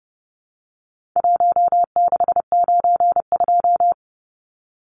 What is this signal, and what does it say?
Morse code, heard as a keyed tone, about 30 wpm, 700 Hz. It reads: 1692
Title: Numbers - 4-Digit 30wpm